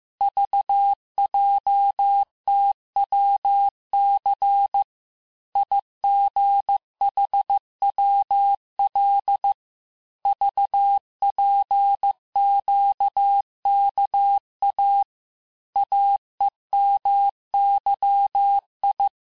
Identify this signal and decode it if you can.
Morse code, heard as a keyed tone, about 15 wpm, 800 Hz. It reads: VJTWC IGHWL VPQKA AEMYI
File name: cours-du-lundi files / cours-1585-texte-chiffre-vitesse-14mpm